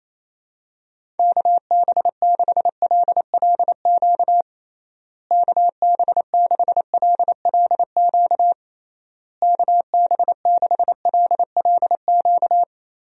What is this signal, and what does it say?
Morse code, heard as a keyed tone, about 28 wpm, 700 Hz. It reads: KB6LLQ KB6LLQ KB6LLQ
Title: USA Call Signs - 6 Characters - Repeated 3x - 28wpm